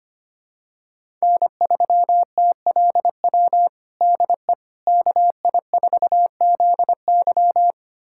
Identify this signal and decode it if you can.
Morse code, heard as a keyed tone, about 25 wpm, 700 Hz. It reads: N3TLW DE KI4ZY
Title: USA Callsign DE Callsign - 5 Characters - 25wpm